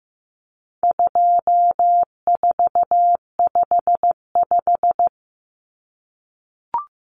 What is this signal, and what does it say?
Morse code, heard as a keyed tone, about 15 wpm, 700 Hz. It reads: 2455
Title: Common Cut Numbers - 4-Digit 15wpm